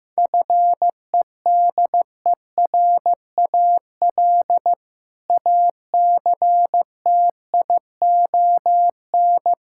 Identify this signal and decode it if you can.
Morse code, heard as a keyed tone, about 15 wpm, 700 Hz. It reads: FEDERAL ACTION
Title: Sentences from Top 1000 Words - Review Part 5 - 15wpm